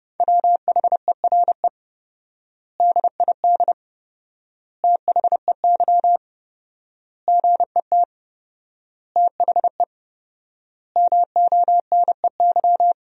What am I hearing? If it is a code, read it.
Morse code, heard as a keyed tone, about 29 wpm, 700 Hz. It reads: WHERE DID THEY GET THE MONEY
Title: 4x Word-Spacing - Sentences from Top 400 Words - 30wpm